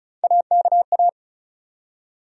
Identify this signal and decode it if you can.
Morse code, heard as a keyed tone, about 36 wpm, 700 Hz. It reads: AKA
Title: USA Call Sign Postfixes - 3 Letters 35wpm